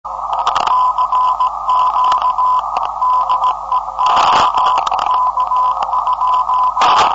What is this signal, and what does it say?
Morse code, heard as a keyed tone, about 17 wpm, 1010 Hz, with noise. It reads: CQ DEX K?
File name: 472khz web swl / 05october2012 / ok2bvg 472khz